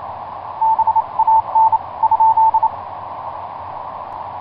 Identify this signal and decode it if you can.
Morse code, heard as a keyed tone, about 26 wpm, 890 Hz. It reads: BAN ?
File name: vk9x-g6ay iv3kvc etc 12m cw 23feb2010 0803